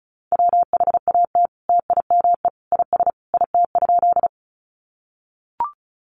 Morse code, heard as a keyed tone, about 34 wpm, 700 Hz. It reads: WHAT TIME IS IT?
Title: Sentences from Top 100 Words 35wpm